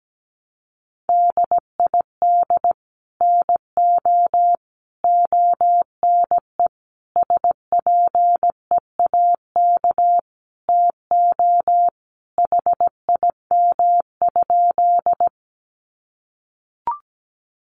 Morse code, heard as a keyed tone, about 17 wpm, 700 Hz. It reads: DID NO ONE SPEAK TO HIM?
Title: Sentences from Top 1100 Words 17wpm